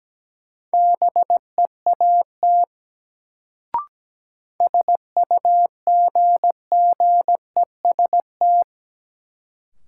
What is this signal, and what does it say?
Morse code, heard as a keyed tone, about 17 wpm, 700 Hz. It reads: BEAT SUGGEST
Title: Sentences from Top 1300 Words - New Word Drill Alt - 17wpm